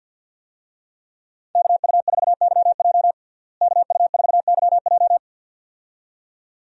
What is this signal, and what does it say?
Morse code, heard as a keyed tone, about 49 wpm, 700 Hz. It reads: KU4YJ KU4YJ
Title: USA Call Signs - 5 Characters - Repeated 2x - 50wpm